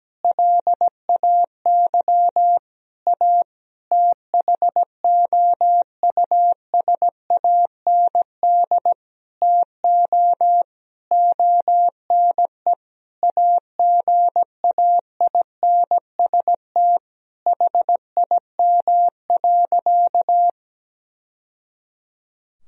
Morse code, heard as a keyed tone, about 17 wpm, 700 Hz. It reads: LAY A THOUSAND TO ONE AGAINST HIM.
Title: Sentences from Top 1000 Words - Review Part 3 - 17wpm